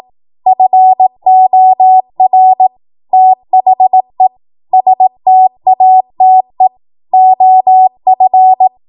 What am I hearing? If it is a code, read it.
Morse code, heard as a keyed tone, about 18 wpm, 750 Hz. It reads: FOR THE STATE OF
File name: Morse / 18WPM